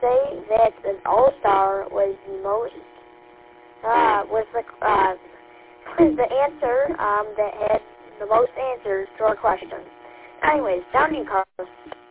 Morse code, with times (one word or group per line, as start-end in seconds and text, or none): none